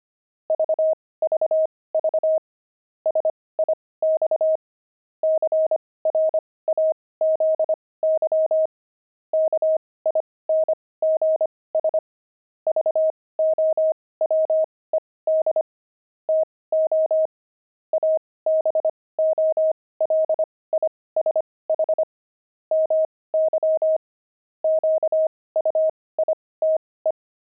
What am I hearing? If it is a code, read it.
Morse code, heard as a keyed tone, about 25 wpm, spaced as for 18 wpm, 630 Hz. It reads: VVV SIX CRAZY KINGS VOWED TO ABOLISH MY QUITE
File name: pangram-fs-18-cs-25-crazy